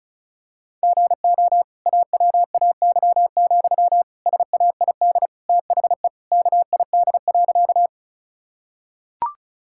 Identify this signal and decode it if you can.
Morse code, heard as a keyed tone, about 36 wpm, 700 Hz. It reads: GO AWAY, SAID THE KID.
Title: Sentences from Top 1600 Words 35wpm